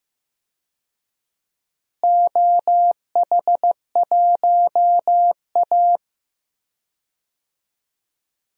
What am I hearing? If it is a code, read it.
Morse code, heard as a keyed tone, about 15 wpm, 700 Hz. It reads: OH1A
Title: DXCC-100 Call Signs - 4 Characters - Repeated 3x - 15wpm